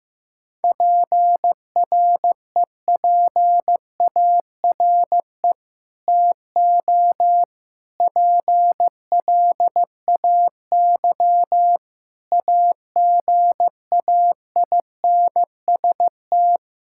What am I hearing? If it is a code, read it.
Morse code, heard as a keyed tone, about 15 wpm, 700 Hz. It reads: PREPARE TO PLAY AGAINST
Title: Sentences from Top 1000 Words - Review Part 6 - 15wpm